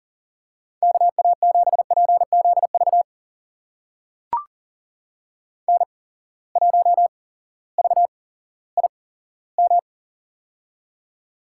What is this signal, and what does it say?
Morse code, heard as a keyed tone, about 40 wpm, spaced as for 18 wpm, 700 Hz. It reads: KA7PZV N1VIM
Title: USA Call Signs - 8x Character Spacing - 40wpm